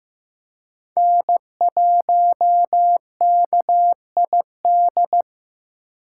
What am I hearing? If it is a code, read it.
Morse code, heard as a keyed tone, about 15 wpm, 700 Hz. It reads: N1KID
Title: USA Call Signs - 2x Character Spacing - 15wpm